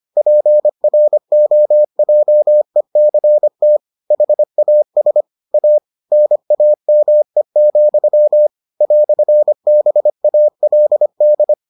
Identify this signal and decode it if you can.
Morse code, heard as a keyed tone, about 25 wpm, 600 Hz. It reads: PROJECT HAS A NAME, "BALD